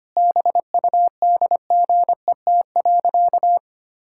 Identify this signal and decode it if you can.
Morse code, heard as a keyed tone, about 25 wpm, 700 Hz. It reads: BUDGET.
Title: Pangram Challenge v2 - 25wpm